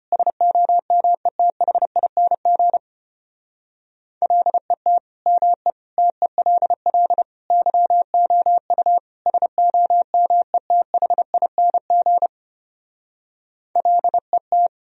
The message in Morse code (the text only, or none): SOMETHING LET ME TELL YOU SOMETHING LET